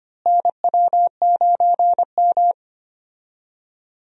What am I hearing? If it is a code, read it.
Morse code, heard as a keyed tone, about 25 wpm, 700 Hz. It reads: NW9M
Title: USA Call Signs - 4 Characters - Repeated 2x - 25wpm